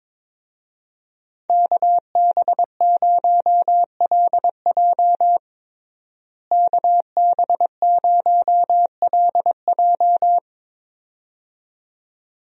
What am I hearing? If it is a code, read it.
Morse code, heard as a keyed tone, about 21 wpm, 700 Hz. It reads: KB0LJ KB0LJ
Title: USA Call Signs - 5 Characters - Repeated 2x - 22wpm